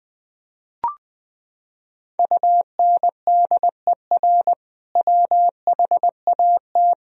Morse code, heard as a keyed tone, about 21 wpm, 700 Hz. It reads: UNDER WHAT
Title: Sentences from Top 1000 Words - Review Part 1 - 20wpm